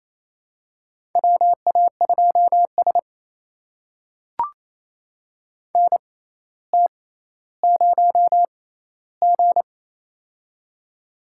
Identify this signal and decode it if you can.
Morse code, heard as a keyed tone, about 28 wpm, spaced as for 14 wpm, 700 Hz. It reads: WA2S NT0G
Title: USA Call Signs - 6x Character Spacing - 28wpm